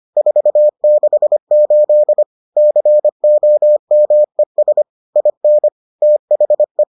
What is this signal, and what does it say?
Morse code, heard as a keyed tone, about 25 wpm, 600 Hz. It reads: 468 COMES IN THE